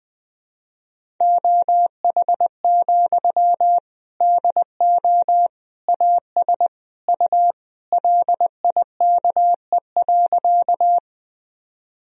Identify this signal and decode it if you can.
Morse code, heard as a keyed tone, about 21 wpm, 700 Hz. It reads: OH, DO AS U LIKE.
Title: Sentences from Top 500 Words and Abbreviations 20wpm